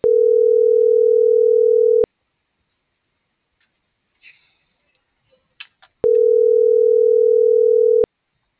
Ringback tone (North America), 440 + 480 Hz.
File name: CENTRAL MORENA / out ruth.skyangel.24